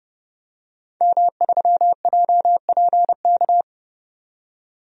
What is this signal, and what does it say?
Morse code, heard as a keyed tone, about 29 wpm, 700 Hz. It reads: M3JPK
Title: DXCC-100 Call Signs - 5 Characters - 30wpm